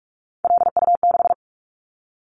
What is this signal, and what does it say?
Morse code, heard as a keyed tone, about 46 wpm, 700 Hz. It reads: LU6
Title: Callsign Prefixes for Top 100 DXCC - 45wpm